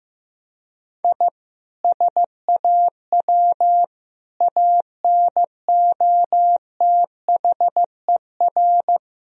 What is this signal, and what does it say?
Morse code, heard as a keyed tone, about 16 wpm, 700 Hz. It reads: I SAW ANOTHER
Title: Sentences from Top 1000 Words - Review Part 3 - 15wpm